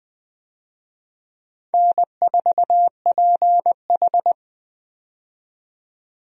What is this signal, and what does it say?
Morse code, heard as a keyed tone, about 20 wpm, 700 Hz. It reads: N4PH
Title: USA Call Signs - 4 Characters - 20wpm